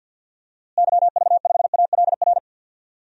Morse code, heard as a keyed tone, about 53 wpm, 700 Hz. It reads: YV5ALR
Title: DXCC-100 Call Signs - 6 Characters - Repeated 2x - 50wpm